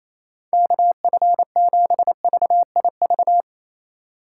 Morse code, heard as a keyed tone, about 28 wpm, 700 Hz. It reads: KF7VIV
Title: USA Call Signs - 6 Characters - 4x Character Spacing - 28wpm